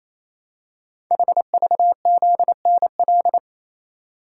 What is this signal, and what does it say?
Morse code, heard as a keyed tone, about 27 wpm, 700 Hz. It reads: HVZNL
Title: Code Groups - 5 Letters - 28wpm